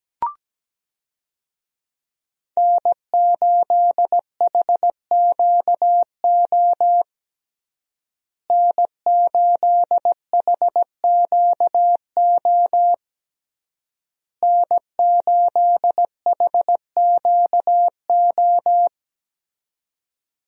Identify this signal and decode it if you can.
Morse code, heard as a keyed tone, about 17 wpm, 700 Hz. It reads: N8HQO N8HQO N8HQO
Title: USA Call Signs - 5 Characters - Repeated 3x - 17wpm